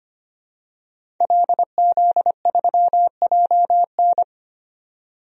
Morse code, heard as a keyed tone, about 25 wpm, 700 Hz. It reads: LZ3JN
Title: DXCC-100 Call Signs - 5 Characters - 25wpm